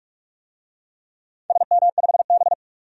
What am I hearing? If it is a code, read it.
Morse code, heard as a keyed tone, about 44 wpm, 700 Hz. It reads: SM5B
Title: DXCC-100 Call Signs - 4 Characters - 45wpm